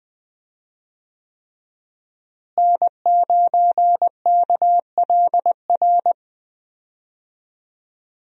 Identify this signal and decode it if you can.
Morse code, heard as a keyed tone, about 20 wpm, 700 Hz. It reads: N9KLR
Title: USA Call Signs - 5 Characters - Repeated 3x - 20wpm